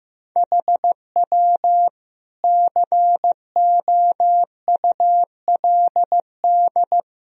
Morse code, heard as a keyed tone, about 15 wpm, 700 Hz. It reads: HW COULD
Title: Sentences from Top 200 Words and Abbreviations 15wpm